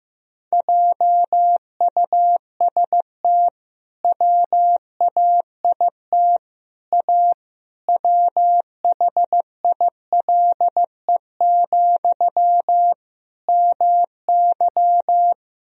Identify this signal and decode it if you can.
Morse code, heard as a keyed tone, about 15 wpm, 700 Hz. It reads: JUST WAIT A WHILE, MY
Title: Sentences from Top 1000 Words - Review Part 1 - 15wpm